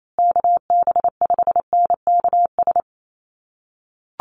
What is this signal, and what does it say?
Morse code, heard as a keyed tone, about 28 wpm, 700 Hz. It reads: KB5NKS
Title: USA Call Signs Repeated 3x 28wpm